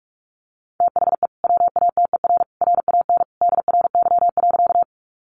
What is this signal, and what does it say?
Morse code, heard as a keyed tone, about 44 wpm, 700 Hz. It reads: THE WATER RAN DRY.
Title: Sentences from Top 500 Words 45wpm